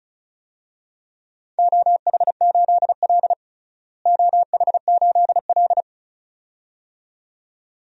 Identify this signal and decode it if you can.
Morse code, heard as a keyed tone, about 36 wpm, 700 Hz. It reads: OH8L OH8L
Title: DXCC-100 Call Signs - 4 Characters - Repeated 2x - 35wpm